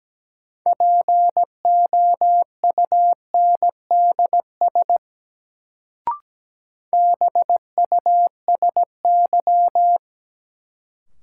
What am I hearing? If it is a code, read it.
Morse code, heard as a keyed tone, about 17 wpm, 700 Hz. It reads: POUNDS BUSY
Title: Sentences from Top 1000 Words - Review Part 2 - New Word Drill - 17wpm